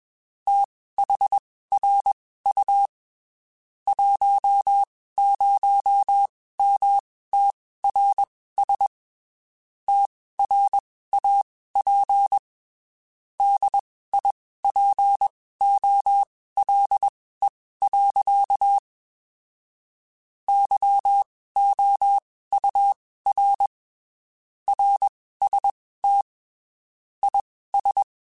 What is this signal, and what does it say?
Morse code, heard as a keyed tone, about 21 wpm, spaced as for 15 wpm, 800 Hz. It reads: THRU 10MTRS TRAP DIPOLE. YOUR RST IS